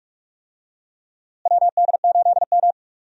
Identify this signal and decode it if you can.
Morse code, heard as a keyed tone, about 45 wpm, 700 Hz. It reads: WD8M
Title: USA Call Signs - 4 Characters - Repeated 3x - 45wpm